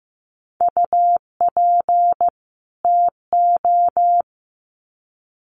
Morse code, heard as a keyed tone, about 15 wpm, 700 Hz. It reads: UP TO
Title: Sets of 2 Words 15wpm